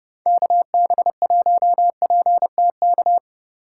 Morse code, heard as a keyed tone, about 30 wpm, 700 Hz. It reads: KB1PTK